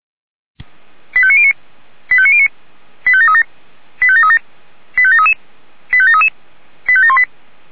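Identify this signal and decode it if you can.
Five-tone selective call (ZVEI), heard as quick runs of five tones, tones of about 70 ms. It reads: 85909 85909 85737 85737 85733 85733 85718
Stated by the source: Five-tone ZVEI selective call 85746, received by SDR.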